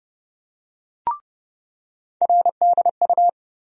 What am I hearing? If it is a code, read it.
Morse code, heard as a keyed tone, about 29 wpm, 700 Hz. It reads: RDU
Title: USA Call Sign Postfixes - 3 Letters 30wpm